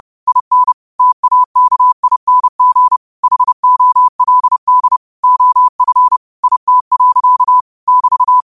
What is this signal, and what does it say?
Morse code, heard as a keyed tone, about 30 wpm, 1000 Hz. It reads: IN TAKING HOLD OF IT. <BT>
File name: qotdfiles / 30wpm